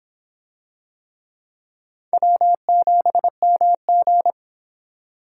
Morse code, heard as a keyed tone, about 24 wpm, 700 Hz. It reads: W7MG